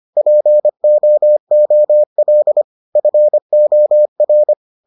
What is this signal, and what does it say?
Morse code, heard as a keyed tone, about 25 wpm, 600 Hz. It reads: POOL FOR